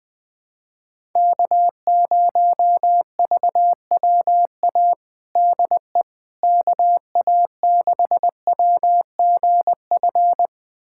Morse code, heard as a keyed tone, about 21 wpm, 700 Hz. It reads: K0VWA DE KA6WGF
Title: USA Callsign DE Callsign 20wpm